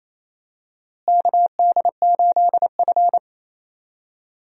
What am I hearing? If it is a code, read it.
Morse code, heard as a keyed tone, about 28 wpm, 700 Hz. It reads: KD8F
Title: USA Call Signs - 4 Characters - Repeated 2x - 28wpm